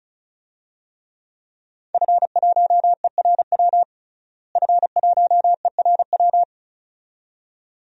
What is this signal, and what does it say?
Morse code, heard as a keyed tone, about 36 wpm, 700 Hz. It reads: F1ERW F1ERW
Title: DXCC-100 Call Signs - 5 Characters - Repeated 2x - 35wpm